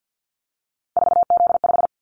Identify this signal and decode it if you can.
Morse code, heard as a keyed tone, about 50 wpm, 700 Hz. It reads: 4Z5
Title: Callsign Prefixes for Top 50 DXCC - 50wpm